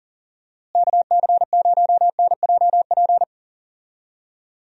Morse code, heard as a keyed tone, about 37 wpm, 700 Hz. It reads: KC0NJP
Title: USA Call Signs - 6 Characters - Repeated 2x - 40wpm